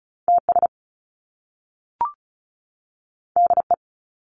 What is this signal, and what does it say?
Morse code, heard as a keyed tone, about 36 wpm, 700 Hz. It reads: TS DE
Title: Top Two Letter Combinations in Words 35wpm